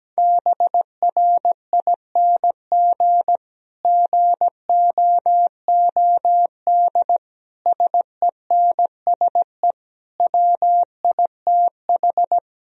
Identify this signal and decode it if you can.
Morse code, heard as a keyed tone, about 17 wpm, 700 Hz. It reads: BRING GOOD SENSE WITH
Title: Sentences from Top 1000 Words - Review Part 4 - 17wpm